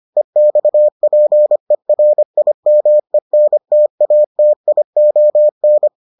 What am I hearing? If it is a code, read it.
Morse code, heard as a keyed tone, about 25 wpm, 600 Hz. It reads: EXPERIMENTATION